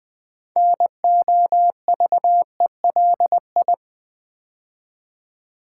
Morse code, heard as a keyed tone, about 20 wpm, 700 Hz. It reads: NOVELI
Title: Build A Word - 2 Letters At A Time - 8-Letter Words - 20wpm